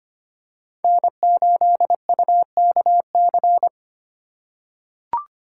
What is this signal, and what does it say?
Morse code, heard as a keyed tone, about 25 wpm, 700 Hz. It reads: N8UKC
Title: USA Call Signs - 5 Characters - 4x Character Spacing - 25wpm